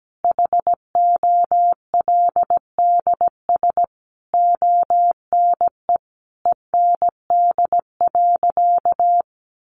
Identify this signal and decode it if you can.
Morse code, heard as a keyed tone, about 17 wpm, 700 Hz. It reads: HOLDS ONE END.
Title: Easy Phrases 17wpm